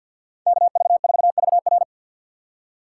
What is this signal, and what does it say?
Morse code, heard as a keyed tone, about 51 wpm, 700 Hz. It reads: KV4VR
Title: USA Call Signs - 5 Characters - 3x Character Spacing - 50wpm